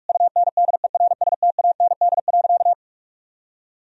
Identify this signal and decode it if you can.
Morse code, heard as a keyed tone, about 41 wpm, 700 Hz. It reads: UNDERSTAND.
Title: Sentences from Top 1000 Words - Review Part 1 - 45wpm